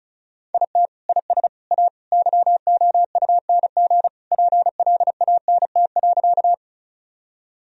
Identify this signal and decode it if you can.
Morse code, heard as a keyed tone, about 36 wpm, 700 Hz. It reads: IT IS A YOUNG PLANT.
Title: Sentences from Top 1000 Words - Review Part 1 - 35wpm